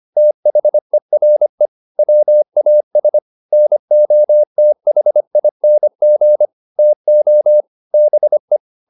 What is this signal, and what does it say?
Morse code, heard as a keyed tone, about 25 wpm, 600 Hz. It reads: THERE WAS NOTHING TO BE